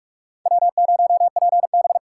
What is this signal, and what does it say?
Morse code, heard as a keyed tone, about 46 wpm, 700 Hz. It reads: W0PB
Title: USA Call Signs - 4 Characters - 2x Character Spacing - 45wpm